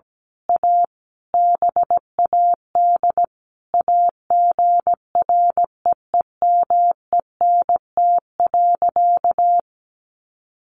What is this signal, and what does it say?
Morse code, heard as a keyed tone, about 17 wpm, 700 Hz. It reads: A BAD AGREEMENT.
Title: Sentences from Top 1500 Words 17wpm